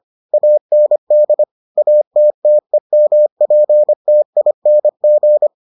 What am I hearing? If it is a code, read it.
Morse code, heard as a keyed tone, about 25 wpm, 600 Hz. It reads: AND ATTEMPTING